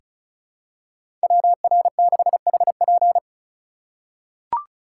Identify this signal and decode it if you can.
Morse code, heard as a keyed tone, about 36 wpm, 700 Hz. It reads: WR6HP
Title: USA Call Signs - 5 Characters - 3x Character Spacing - 35wpm